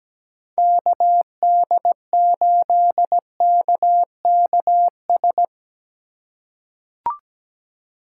Morse code, heard as a keyed tone, about 17 wpm, 700 Hz. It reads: KD8KKS
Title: USA Call Signs - 6 Characters - 3x Character Spacing - 17wpm